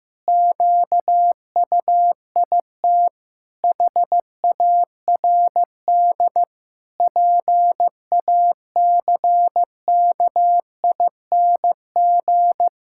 Morse code, heard as a keyed tone, about 15 wpm, 700 Hz. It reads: QUIT HARD PACKING
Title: Pangram Challenge v2 - 15wpm